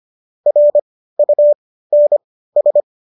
Morse code, heard as a keyed tone, about 25 wpm, spaced as for 15 wpm, 590 Hz. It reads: RUNS